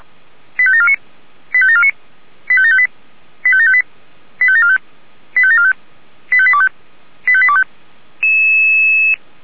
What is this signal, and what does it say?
Five-tone selective call (ZVEI), heard as quick runs of five tones, tones of about 70 ms. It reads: 85749 85749 85758 85758 85745 85745 85825 85825
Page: ZVEI